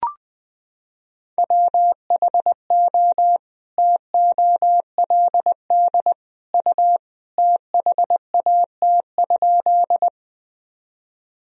Morse code, heard as a keyed tone, about 20 wpm, 700 Hz. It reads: WHO TOLD U THAT?
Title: Sentences from Top 400 Words and Abbreviations 20wpm